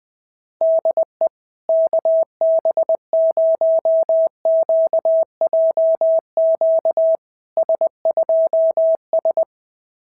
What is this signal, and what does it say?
Morse code, heard as a keyed tone, about 20 wpm, 650 Hz. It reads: DE KB0QJQ S2S
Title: SOTA QSO Practice 20wpm